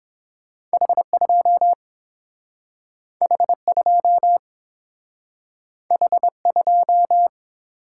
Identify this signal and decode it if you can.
Morse code, heard as a keyed tone, about 30 wpm, 700 Hz. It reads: H 2 H 2 H 2